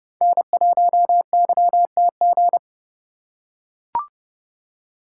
Morse code, heard as a keyed tone, about 30 wpm, 700 Hz. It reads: N1YTG
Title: USA Call Signs - 5 Characters - 2x Character Spacing - 30wpm